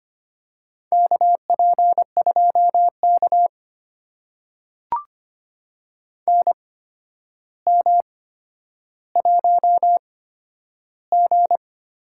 Morse code, heard as a keyed tone, about 25 wpm, spaced as for 11 wpm, 700 Hz. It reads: KP2K NM1G
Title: USA Call Signs - 4 Characters - 8x Character Spacing - 25wpm